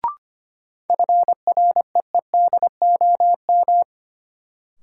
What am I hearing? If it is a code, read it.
Morse code, heard as a keyed tone, about 25 wpm, 700 Hz. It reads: FREEDOM
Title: Sentences from Top 800 Words - New Word Drill - 25wpm